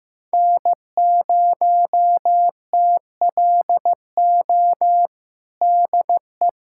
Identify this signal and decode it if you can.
Morse code, heard as a keyed tone, about 15 wpm, 700 Hz. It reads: N0TLO DE
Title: USA Callsign DE Callsign - 5 Characters - 15wpm